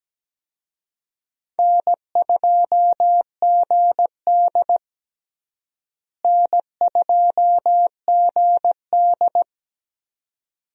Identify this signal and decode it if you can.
Morse code, heard as a keyed tone, about 17 wpm, 700 Hz. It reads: N2GD N2GD
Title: USA Call Signs - 4 Characters - Repeated 2x - 17wpm